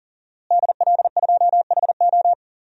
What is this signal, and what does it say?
Morse code, heard as a keyed tone, about 40 wpm, 690 Hz. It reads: DL2HO
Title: DXCC-100 Call Signs - 5 Characters - Repeated 2x - 40wpm